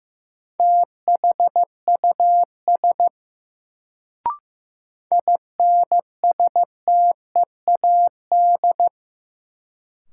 Morse code, heard as a keyed tone, about 16 wpm, 700 Hz. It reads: THUS INSTEAD
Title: Sentences from Top 800 Words - New Word Drill - 15wpm